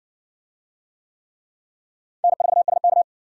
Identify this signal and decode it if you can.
Morse code, heard as a keyed tone, about 60 wpm, 700 Hz. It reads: N4SK